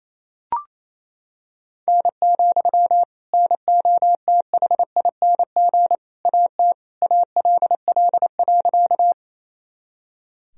Morse code, heard as a keyed tone, about 28 wpm, 700 Hz. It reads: N, NOTHING AT ALL.
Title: Sentences from Top 500 Words and Abbreviations 28wpm